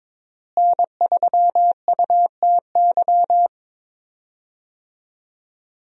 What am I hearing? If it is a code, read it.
Morse code, heard as a keyed tone, about 22 wpm, 700 Hz. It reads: N3UTY
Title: USA Call Signs - 5 Characters - Repeated 2x - 22wpm